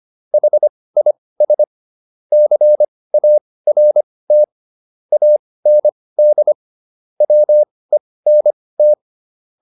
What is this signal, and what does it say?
Morse code, heard as a keyed tone, about 25 wpm, spaced as for 18 wpm, 600 Hz. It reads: HIS CART AND WENT